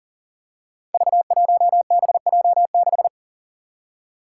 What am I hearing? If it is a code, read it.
Morse code, heard as a keyed tone, about 40 wpm, 700 Hz. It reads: V1BJ6
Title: Five Challenging Letter-Number Combinations 40wpm